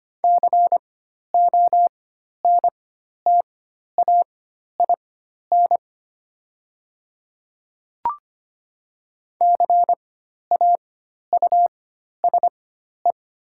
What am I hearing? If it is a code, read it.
Morse code, heard as a keyed tone, about 25 wpm, spaced as for 12 wpm, 690 Hz. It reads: CONTAIN CAUSE
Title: 4x Character Spacing - Top 500 Words - 25wpm